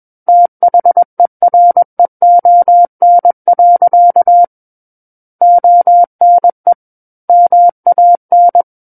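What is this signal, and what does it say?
Morse code, heard as a keyed tone, about 21 wpm, 700 Hz. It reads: THEREON. ONE MAN
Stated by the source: Morse code practice reading of a book chapter at 21 words per minute.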